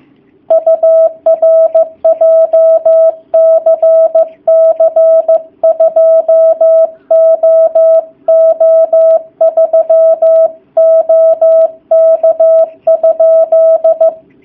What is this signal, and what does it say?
Morse code, heard as a keyed tone, about 15 wpm, 650 Hz. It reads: UR JCC 2OO3 OK?
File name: jh8cbh / 2019cw / sp15- w